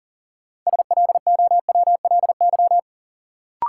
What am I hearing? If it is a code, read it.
Morse code, heard as a keyed tone, about 40 wpm, 700 Hz. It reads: SLOWLY
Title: Sentences from Top 1100 Words - New Word Drill - 40wpm